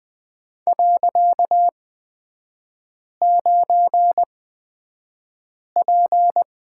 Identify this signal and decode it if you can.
Morse code, heard as a keyed tone, about 20 wpm, spaced as for 5 wpm, 700 Hz. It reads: .9P
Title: 255 Lesson 41e - Period - Three Characters 20wpm